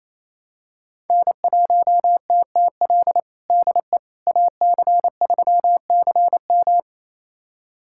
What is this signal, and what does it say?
Morse code, heard as a keyed tone, about 28 wpm, 700 Hz. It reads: N1TTL DE AC3CM
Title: USA Callsign DE Callsign - 5 Characters - 28wpm